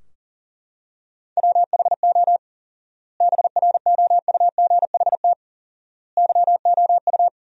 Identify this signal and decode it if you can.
Morse code, heard as a keyed tone, about 40 wpm, 700 Hz. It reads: WHO BROUGHT YOU
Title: 4x Word-Spacing - Sentences from Top 500 Words - 40wpm